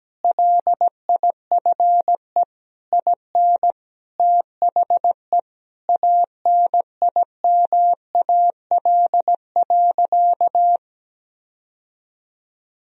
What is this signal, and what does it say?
Morse code, heard as a keyed tone, about 17 wpm, 700 Hz. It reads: LIFE IN THE ANIMAL.
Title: Sentences from Top 1000 Words - Review Part 1 - 17wpm